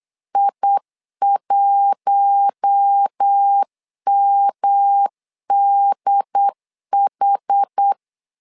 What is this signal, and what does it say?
Morse code, heard as a keyed tone, about 8 wpm, 800 Hz. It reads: I1MDH
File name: LEZIONI / 4 Q1 8wpm 3035